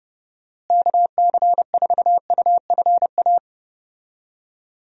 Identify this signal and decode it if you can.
Morse code, heard as a keyed tone, about 30 wpm, 700 Hz. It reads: KC4UFA
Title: USA Call Signs - 6 Characters - Repeated 2x - 30wpm